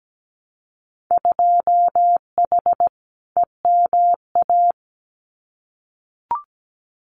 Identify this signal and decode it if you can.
Morse code, heard as a keyed tone, about 17 wpm, 700 Hz. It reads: 2H EMA
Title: Winter Field Day Exchanges 17wpm